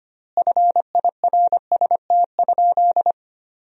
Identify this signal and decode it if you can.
Morse code, heard as a keyed tone, about 25 wpm, 700 Hz. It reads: FIRST?
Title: 3x Word-Spacing - Sentences from Top 100 Words - 25wpm